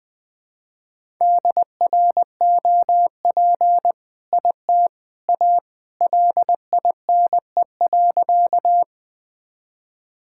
Morse code, heard as a keyed tone, about 20 wpm, 700 Hz. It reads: DROP IT A LINE.
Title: All-Age Appropriate Jokes - 20wpm